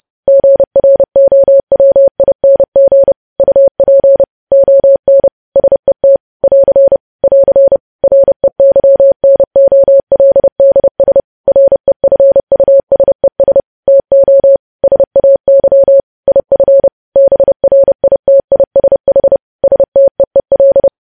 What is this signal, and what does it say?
Morse code, heard as a keyed tone, about 30 wpm, 550 Hz. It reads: GROWING UP ON SET <AR> <AR> REYNOLDS REFUSES TO SAY IF BRITISH STEEL
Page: news headlines in morse code at 30 wpm